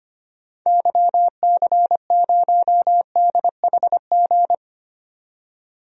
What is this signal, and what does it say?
Morse code, heard as a keyed tone, about 25 wpm, 700 Hz. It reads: YC0DHG
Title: DXCC-100 Call Signs - 6 Characters - 25wpm